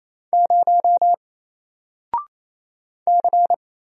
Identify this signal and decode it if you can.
Morse code, heard as a keyed tone, about 27 wpm, 700 Hz. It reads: 0 C
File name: Single-Letter-Number-28wpm